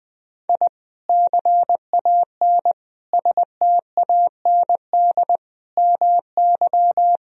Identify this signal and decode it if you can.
Morse code, heard as a keyed tone, about 20 wpm, 700 Hz. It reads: I CAN STAND MY
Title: Sentences from Top 1000 Words - Review Part 2 - 20wpm